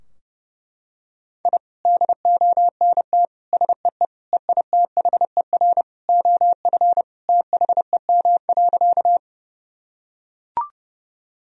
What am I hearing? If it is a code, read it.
Morse code, heard as a keyed tone, about 31 wpm, 700 Hz. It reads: I DONT SEE EITHER OF THEM.
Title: Sentences from Top 1000 Words - Review Part 4 - 30wpm